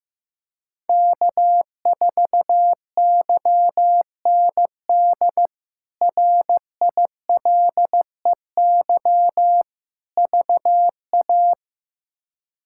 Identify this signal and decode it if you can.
Morse code, heard as a keyed tone, about 15 wpm, 700 Hz. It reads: K4YND RILEY VA
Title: K1USN SST QSO Practice - 15wpm